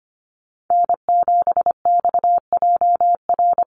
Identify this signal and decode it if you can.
Morse code, heard as a keyed tone, about 25 wpm, 700 Hz. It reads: N7XJR
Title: USA Call Signs Repeated 3x 25wpm